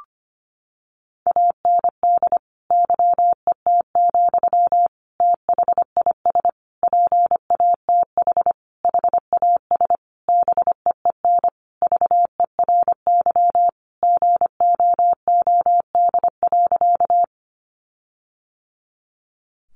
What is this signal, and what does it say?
Morse code, heard as a keyed tone, about 25 wpm, 700 Hz. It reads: AND YET, THIS PATH HAS BEEN VERY GOOD.
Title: Sentences from Top 1400 Words 25wpm